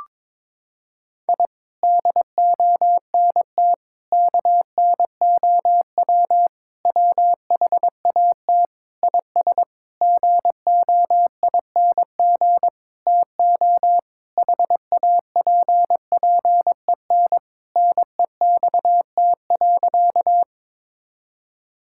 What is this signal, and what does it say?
Morse code, heard as a keyed tone, about 22 wpm, 700 Hz. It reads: I DONT KNOW WHAT IS GOING TO HAPPEN NEXT.
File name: Sentences with 7-Word Phrases - 22wpm